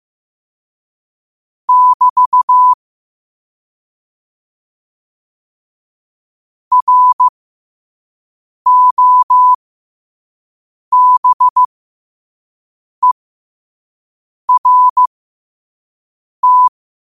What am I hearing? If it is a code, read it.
Morse code, heard as a keyed tone, about 15 wpm, spaced as for 5 wpm, 1000 Hz. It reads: <BT> ROBERT
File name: qotdfiles / 5wpm